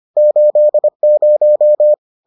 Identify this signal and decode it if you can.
Morse code, heard as a keyed tone, about 25 wpm, 600 Hz. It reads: 80